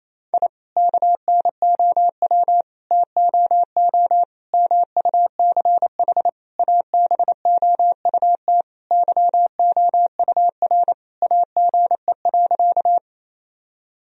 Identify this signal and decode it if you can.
Morse code, heard as a keyed tone, about 28 wpm, 700 Hz. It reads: I KNOW TOO MUCH ABOUT YOUR AGE.
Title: Sentences from Top 1000 Words - Review Part 3 - 28wpm